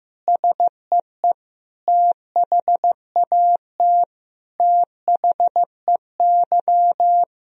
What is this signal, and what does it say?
Morse code, heard as a keyed tone, about 16 wpm, 700 Hz. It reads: SEE THAT THEY
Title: Sentences from Top 1000 Words - Review Part 2 - 15wpm